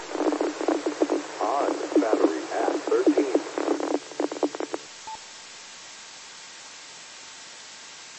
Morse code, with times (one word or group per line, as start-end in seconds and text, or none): none